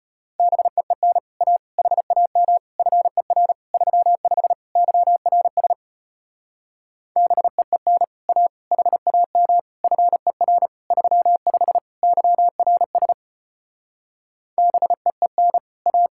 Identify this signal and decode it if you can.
Morse code, heard as a keyed tone, about 33 wpm, 700 Hz. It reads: BEEN A HAM FER 35 YRS BEEN A HAM FER 35 YRS BEEN A